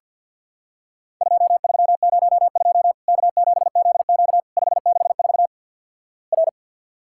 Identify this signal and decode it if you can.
Morse code, heard as a keyed tone, about 50 wpm, 700 Hz. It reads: 2302 K7ZQ 564 R
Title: Worked All Europe DX Contest - QTC Exchange - 50wpm